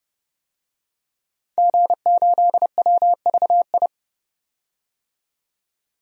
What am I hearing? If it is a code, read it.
Morse code, heard as a keyed tone, about 30 wpm, 700 Hz. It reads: G8WVI
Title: DXCC-100 Call Signs - 5 Characters - 30wpm